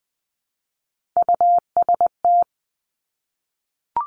Morse code, heard as a keyed tone, about 20 wpm, 700 Hz. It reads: UST
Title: Top Three Letter Combinations in Words 20wpm